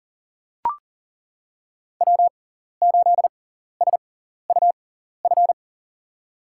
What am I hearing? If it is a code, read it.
Morse code, heard as a keyed tone, about 40 wpm, spaced as for 14 wpm, 700 Hz. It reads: W8SUF
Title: USA Call Signs - 5 Characters - 6x Character Spacing - 40wpm